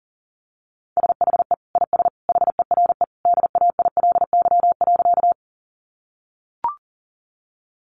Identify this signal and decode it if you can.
Morse code, heard as a keyed tone, about 40 wpm, 700 Hz. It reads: SHE IS HERE DAILY.
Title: Sentences from Top 1100 Words 40wpm